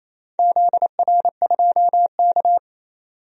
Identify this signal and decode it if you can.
Morse code, heard as a keyed tone, about 28 wpm, 700 Hz. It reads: ZR2K
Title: DXCC-100 Call Signs - 4 Characters - 28wpm